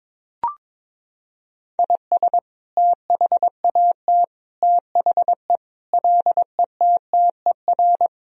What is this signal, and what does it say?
Morse code, heard as a keyed tone, about 22 wpm, 700 Hz. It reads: IS THAT THE LETTER
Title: Sentences from Top 1000 Words - Review Part 2 - 22wpm